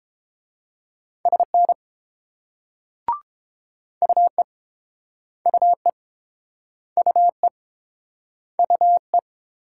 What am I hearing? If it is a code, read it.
Morse code, heard as a keyed tone, about 30 wpm, 690 Hz. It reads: SN UE UE UE UE